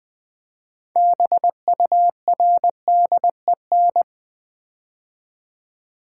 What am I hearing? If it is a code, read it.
Morse code, heard as a keyed tone, about 20 wpm, 700 Hz. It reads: BURDEN
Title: Build A Word - 3 Letters At A Time - 6-Letter Words - 20wpm